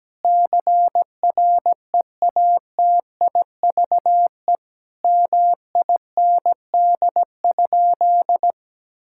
Morse code, heard as a keyed tone, about 17 wpm, 700 Hz. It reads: CREATIVE MIND?
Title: Sentences from Top 1000 Words - Review Part 5 - 17wpm